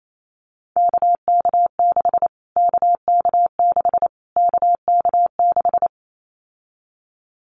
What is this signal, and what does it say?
Morse code, heard as a keyed tone, about 28 wpm, 700 Hz. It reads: KK6 KK6 KK6
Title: Callsign Prefixes for Top 25 DXCC - Repeated 3x - 28wpm